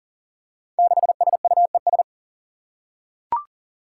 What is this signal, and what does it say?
Morse code, heard as a keyed tone, about 37 wpm, 700 Hz. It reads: 6SUES
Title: Five Challenging Letter-Number Combinations 40wpm